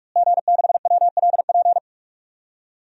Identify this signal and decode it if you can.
Morse code, heard as a keyed tone, about 45 wpm, 700 Hz. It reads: G6WLP
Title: DXCC-100 Call Signs - 5 Characters - Repeated 2x - 45wpm